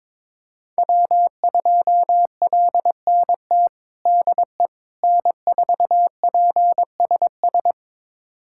Morse code, heard as a keyed tone, about 22 wpm, 700 Hz. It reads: W2LNT DE N4PSS
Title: USA Callsign DE Callsign - 5 Characters - 22wpm